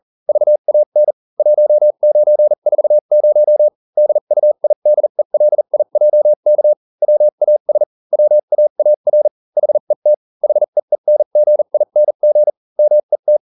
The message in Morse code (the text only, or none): VAN 1940 DUIDELIJK WAS WAAR HET HEENGING MET